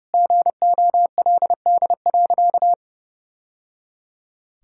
Morse code, heard as a keyed tone, about 32 wpm, 700 Hz. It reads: GOLD.
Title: Sentences from Top 1000 Words - Review Part 6 - 30wpm